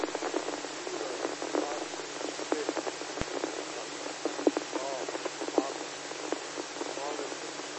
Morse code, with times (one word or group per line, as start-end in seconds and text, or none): none